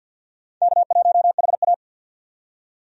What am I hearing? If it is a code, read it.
Morse code, heard as a keyed tone, about 54 wpm, 700 Hz. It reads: K1HA
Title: USA Call Signs - 4 Characters - 50wpm